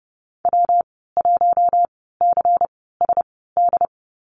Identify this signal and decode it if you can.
Morse code, heard as a keyed tone, about 30 wpm, spaced as for 17 wpm, 700 Hz. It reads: W1CSD
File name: Lesson 19g - Letter w - Call Signs 30wpm